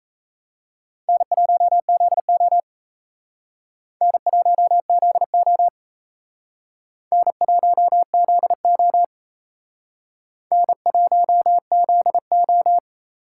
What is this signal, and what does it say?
Morse code, heard as a keyed tone, about 35 wpm, 700 Hz. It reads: N1ZO N1ZO N1ZO N1ZO
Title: Speed Racing - USA 1x2 Call Signs 28wpm